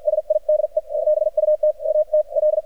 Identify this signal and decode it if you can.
Morse code, heard as a keyed tone, about 53 wpm, spaced as for 44 wpm, 610 Hz, with radio noise. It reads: SIDE BUT AT L
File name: archived / CRSnet